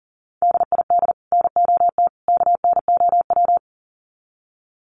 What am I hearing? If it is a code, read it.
Morse code, heard as a keyed tone, about 40 wpm, 700 Hz. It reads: DID NOT KNOW
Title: Sets of 3 Words 40wpm